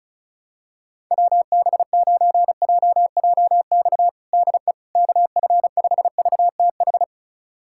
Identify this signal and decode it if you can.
Morse code, heard as a keyed tone, about 36 wpm, 700 Hz. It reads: WB9JJX DE KF5VTH
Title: USA Callsign DE Callsign - 6 Characters - 35wpm